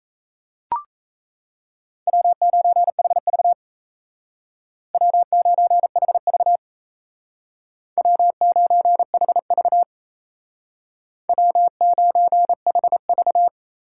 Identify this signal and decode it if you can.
Morse code, heard as a keyed tone, about 36 wpm, 700 Hz. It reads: W9HV W9HV W9HV W9HV